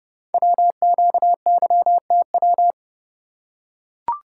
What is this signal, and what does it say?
Morse code, heard as a keyed tone, about 28 wpm, 700 Hz. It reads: WQYTW
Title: Code Groups - 5 Letters - 30wpm